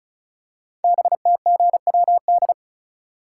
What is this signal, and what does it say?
Morse code, heard as a keyed tone, about 36 wpm, 690 Hz. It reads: BTGWD
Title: Code Groups - 5 Letters - 35wpm